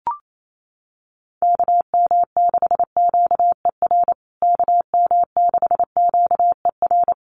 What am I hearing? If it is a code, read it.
Morse code, heard as a keyed tone, about 28 wpm, 690 Hz. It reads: KM6QER KM6QER
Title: USA Call Signs Repeated 3x 28wpm